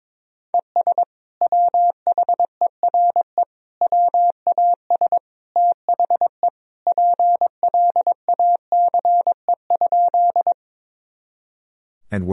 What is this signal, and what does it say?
Morse code, heard as a keyed tone, about 22 wpm, 700 Hz. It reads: ES WHERE WAS THE PLACE?
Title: Sentences from Top 200 Words and Abbreviations 22wpm